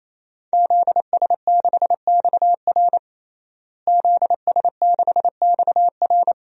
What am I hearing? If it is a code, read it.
Morse code, heard as a keyed tone, about 28 wpm, 700 Hz. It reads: ZS6XR ZS6XR
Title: DXCC-100 Call Signs - 5 Characters - Repeated 3x - 28wpm